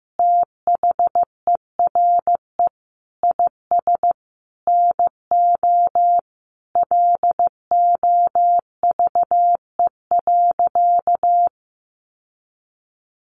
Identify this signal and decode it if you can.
Morse code, heard as a keyed tone, about 15 wpm, 700 Hz. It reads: THERE IS NO LOVE.
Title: Sentences from Top 1200 Words 15wpm